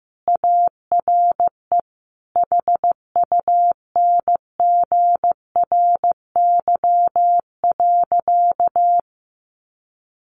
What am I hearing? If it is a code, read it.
Morse code, heard as a keyed tone, about 15 wpm, 700 Hz. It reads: ARE HUNGRY.
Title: A Sound Of Thunder 15wpm